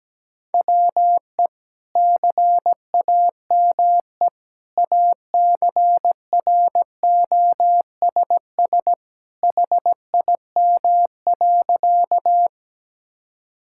Morse code, heard as a keyed tone, about 17 wpm, 700 Hz. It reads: WE CAME ACROSS HIM.
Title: Sentences from Top 1000 Words - Review Part 5 - 17wpm